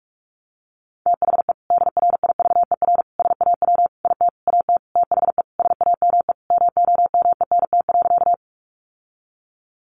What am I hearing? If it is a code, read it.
Morse code, heard as a keyed tone, about 44 wpm, 700 Hz. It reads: THE DRIVER SAW IT AT THE SAME MOMENT.
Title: Sentences from Top 1000 Words 45wpm